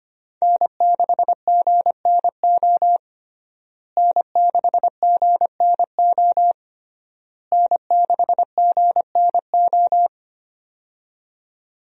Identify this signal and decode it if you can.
Morse code, heard as a keyed tone, about 25 wpm, 700 Hz. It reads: N6GNO N6GNO N6GNO
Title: USA Call Signs - 5 Characters - Repeated 3x - 25wpm